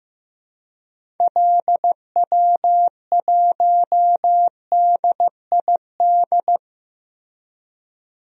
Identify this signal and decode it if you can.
Morse code, heard as a keyed tone, about 15 wpm, 700 Hz. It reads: LW1DID
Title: DXCC-100 Call Signs - 6 Characters - 15wpm